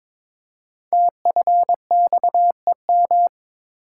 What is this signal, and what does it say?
Morse code, heard as a keyed tone, about 23 wpm, 700 Hz. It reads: TFXEM
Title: Code Groups - 5 Letters - 22wpm